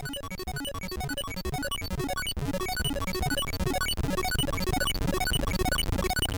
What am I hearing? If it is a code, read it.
There is no code.